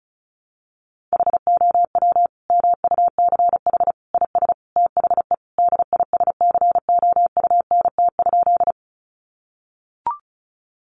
Morse code, heard as a keyed tone, about 34 wpm, 700 Hz. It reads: HOW MUCH IS THE DISCOUNT?
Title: Sentences from Top 1200 Words 35wpm